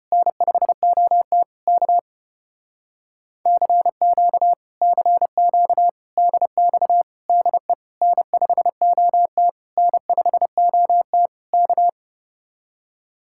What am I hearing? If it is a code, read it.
Morse code, heard as a keyed tone, about 33 wpm, 700 Hz. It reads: N5OT K CQ CQ DX DE N5OT N5OT K